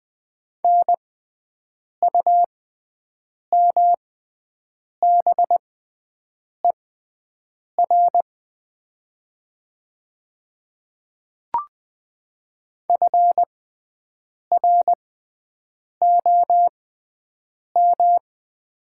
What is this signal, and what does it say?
Morse code, heard as a keyed tone, about 20 wpm, spaced as for 7 wpm, 700 Hz. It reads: NUMBER FROM
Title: 6x Character Spacing - Top 200 Words - 20wpm